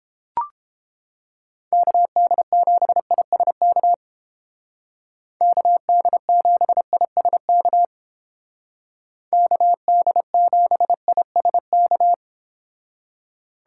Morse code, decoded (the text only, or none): KD7ISK KD7ISK KD7ISK